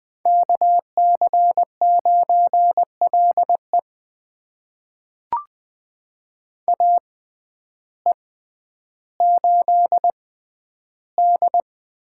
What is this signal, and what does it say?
Morse code, heard as a keyed tone, about 20 wpm, spaced as for 11 wpm, 700 Hz. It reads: KC9LE AE8D
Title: USA Call Signs - 5 Characters - 6x Character Spacing - 20wpm